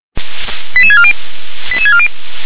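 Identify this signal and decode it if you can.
Five-tone selective call (ZVEI), heard as quick runs of five tones, tones of about 70 ms. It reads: 88640 88640
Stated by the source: ZVEI five-tone selective call received by SDR.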